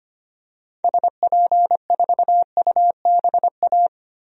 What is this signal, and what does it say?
Morse code, heard as a keyed tone, about 25 wpm, 700 Hz. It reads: SP4UBA
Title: DXCC-100 Call Signs - 6 Characters - Repeated 2x - 25wpm